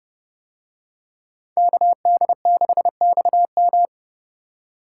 Morse code, heard as a keyed tone, about 30 wpm, 700 Hz. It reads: KD6XM